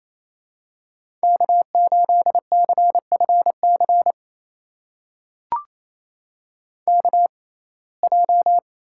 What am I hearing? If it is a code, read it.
Morse code, heard as a keyed tone, about 28 wpm, spaced as for 20 wpm, 700 Hz. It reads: K8CFC KJ
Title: USA Call Signs - 6x Character Spacing - 28wpm